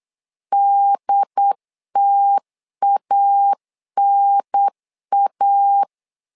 Morse code, heard as a keyed tone, about 8 wpm, 800 Hz. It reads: DTANA
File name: LEZIONI / 3 MWRO / 3 M 8wpm 2300